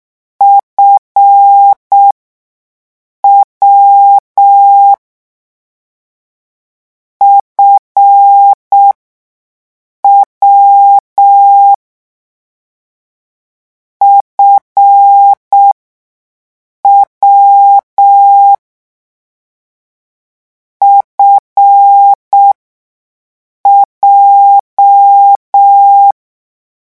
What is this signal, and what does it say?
Morse code, heard as a keyed tone, about 6 wpm, spaced as for 5 wpm, 800 Hz. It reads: FW FW FW FJ
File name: morse mp3 / lecon7-vitesse 6